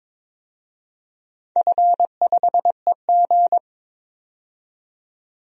Morse code, heard as a keyed tone, about 22 wpm, 700 Hz. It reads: F5EG
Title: DXCC-100 Call Signs - 4 Characters - 22wpm